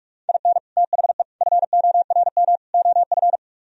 Morse code, heard as a keyed tone, about 47 wpm, 700 Hz. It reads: IN THE FORM OF